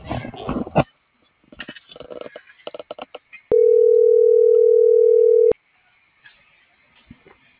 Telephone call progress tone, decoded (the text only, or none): ringback tone